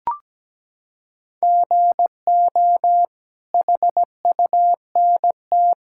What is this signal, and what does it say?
Morse code, heard as a keyed tone, about 17 wpm, 700 Hz. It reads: GO HUNT
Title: Sentences from Top 1600 Words 17wpm